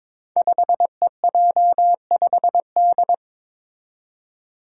Morse code, heard as a keyed tone, about 22 wpm, 700 Hz. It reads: 5EJ5D
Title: Five Challenging Letter-Number Combinations 22wpm